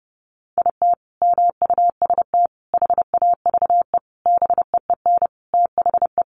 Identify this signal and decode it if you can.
Morse code, heard as a keyed tone, about 28 wpm, 700 Hz. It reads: IT MUST HAVE BEEN THE
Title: Sentences from Top 800 Words 30wpm